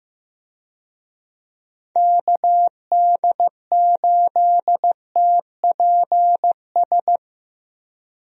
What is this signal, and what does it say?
Morse code, heard as a keyed tone, about 15 wpm, 700 Hz. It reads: KD8TPS
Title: USA Call Signs - 6 Characters - Repeated 2x - 15wpm